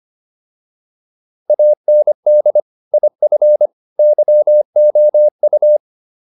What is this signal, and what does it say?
Morse code, heard as a keyed tone, about 25 wpm, 600 Hz. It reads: AND IF YOU
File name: WD205 25